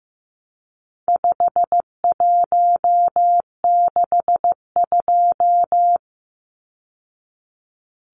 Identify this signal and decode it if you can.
Morse code, heard as a keyed tone, about 15 wpm, 700 Hz. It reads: 5162
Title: Numbers - 4-Digit 15wpm